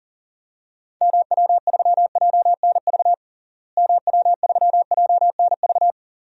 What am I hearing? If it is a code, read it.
Morse code, heard as a keyed tone, about 40 wpm, 700 Hz. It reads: MW3JNV MW3JNV
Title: DXCC-100 Call Signs - 6 Characters - Repeated 3x - 40wpm